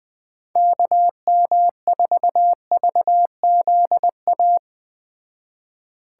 Morse code, heard as a keyed tone, about 20 wpm, 700 Hz. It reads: KM4VZA
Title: Speed Racing - USA 2x3 Call Signs 17wpm